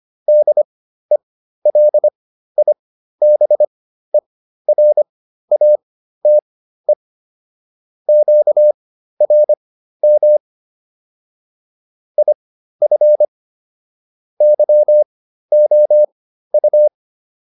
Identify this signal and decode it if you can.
Morse code, heard as a keyed tone, about 25 wpm, spaced as for 13 wpm, 600 Hz. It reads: DELIBERATE QRM IF YOU